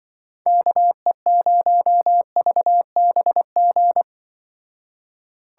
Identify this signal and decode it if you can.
Morse code, heard as a keyed tone, about 23 wpm, 700 Hz. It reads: KE0VBG
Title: Speed Racing - USA Call Signs 20wpm